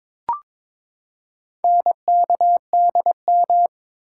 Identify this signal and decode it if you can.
Morse code, heard as a keyed tone, about 23 wpm, 700 Hz. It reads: NKDM
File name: practice / 22wpm / Four Letter-Number Combinations - 22wpm